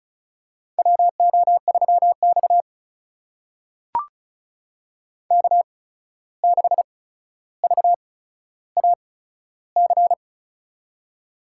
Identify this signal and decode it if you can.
Morse code, heard as a keyed tone, about 36 wpm, spaced as for 14 wpm, 700 Hz. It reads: WO3X K6VAC
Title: USA Call Signs - 8x Character Spacing - 35wpm